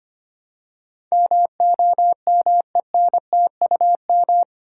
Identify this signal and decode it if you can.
Morse code, heard as a keyed tone, about 25 wpm, 700 Hz. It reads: MOMENTUM
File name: 8 Letter Words - First 3 Letters Given - 25wpm